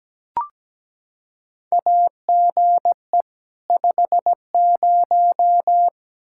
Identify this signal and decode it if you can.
Morse code, heard as a keyed tone, about 17 wpm, 700 Hz. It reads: AGE 50
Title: K1USN SST QSO Practice 17wpm